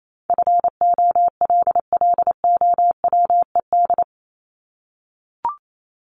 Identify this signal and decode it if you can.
Morse code, heard as a keyed tone, about 28 wpm, 700 Hz. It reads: FOLLOWED
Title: ED Suffix - Top 100 Words with Base Words 28wpm